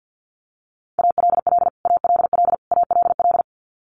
Morse code, heard as a keyed tone, about 48 wpm, 700 Hz. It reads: ALL ALL ALL
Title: Top 300 Words Repeated 50wpm